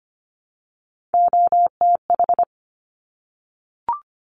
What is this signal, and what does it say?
Morse code, heard as a keyed tone, about 25 wpm, 700 Hz. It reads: OTH
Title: Top Three Letter Combinations in Words 25wpm